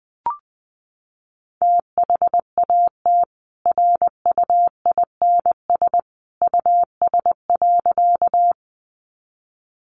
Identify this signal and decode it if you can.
Morse code, heard as a keyed tone, about 20 wpm, 700 Hz. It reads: THAT RUINS US.
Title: A Sound Of Thunder 20wpm